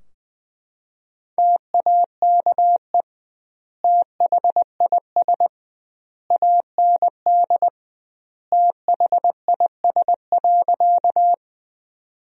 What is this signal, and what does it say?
Morse code, heard as a keyed tone, about 20 wpm, 700 Hz. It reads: TAKE THIS AND THIS.
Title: 2x Word-Spacing - Sentences from Top 200 Words - 20wpm